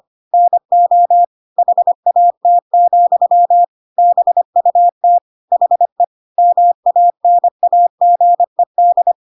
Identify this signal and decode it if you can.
Morse code, heard as a keyed tone, about 25 wpm, 700 Hz. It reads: NO HAT, BUT HE MANAGED